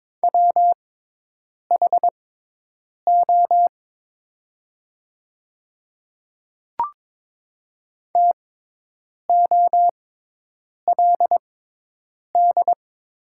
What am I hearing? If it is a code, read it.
Morse code, heard as a keyed tone, about 22 wpm, spaced as for 8 wpm, 700 Hz. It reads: WHO TOLD
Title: 6x Character Spacing - Top 500 Words - 22wpm